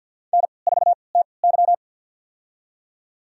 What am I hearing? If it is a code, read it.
Morse code, heard as a keyed tone, about 48 wpm, spaced as for 28 wpm, 700 Hz. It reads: N4TY
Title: USA Call Signs - 4 Characters - 3x Character Spacing - 50wpm